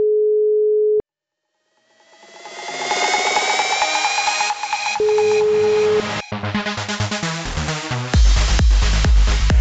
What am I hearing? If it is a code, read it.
Ringback tone (Europe, 425 Hz), with noise.